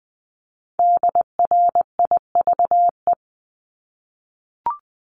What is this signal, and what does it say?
Morse code, heard as a keyed tone, about 20 wpm, 700 Hz. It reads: DRIVE
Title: Multi-letter Phonemes Repeated 3x 20wpm